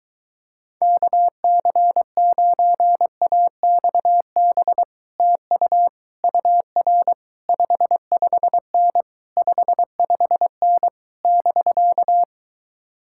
Morse code, heard as a keyed tone, about 22 wpm, 700 Hz. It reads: KC9AXB TU UR 55N 55N <BK>